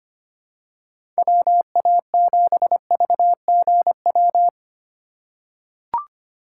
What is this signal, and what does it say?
Morse code, heard as a keyed tone, about 25 wpm, 700 Hz. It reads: WA7VGW
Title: USA Call Signs - 6 Characters - 4x Character Spacing - 25wpm